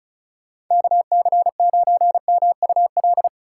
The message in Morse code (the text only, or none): KC9MUL